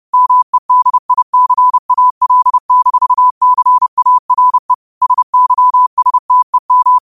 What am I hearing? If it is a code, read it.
Morse code, heard as a keyed tone, about 30 wpm, 1000 Hz. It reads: MEDICAL-CARE SYSTEM